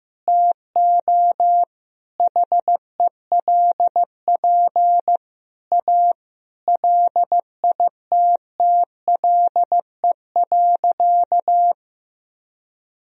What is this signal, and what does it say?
Morse code, heard as a keyed tone, about 15 wpm, 700 Hz. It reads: TO HELP A LITTLE.
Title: Sentences from Top 1000 Words - Review Part 2 - 15wpm